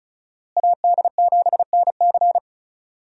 Morse code, heard as a keyed tone, about 36 wpm, 700 Hz. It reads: AD7NC
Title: USA Call Signs - 5 Characters - Repeated 3x - 35wpm